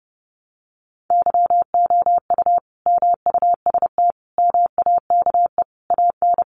Morse code, heard as a keyed tone, about 30 wpm, 700 Hz. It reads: YOU MUST MAKE AN
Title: Sentences from Top 1100 Words 30wpm